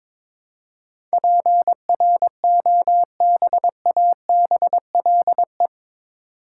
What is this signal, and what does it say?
Morse code, heard as a keyed tone, about 22 wpm, 700 Hz. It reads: PROBABLE
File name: practice / 22wpm / Eight Letter Words - 22wpm